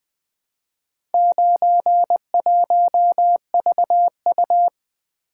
Morse code, heard as a keyed tone, about 20 wpm, 700 Hz. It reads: 91VU
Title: Four Letter-Number Combinations - 20wpm